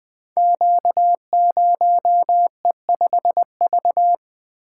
Morse code, heard as a keyed tone, about 21 wpm, 700 Hz. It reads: Q0E5V
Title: Code Groups - 5 Letter:Numbers - 20wpm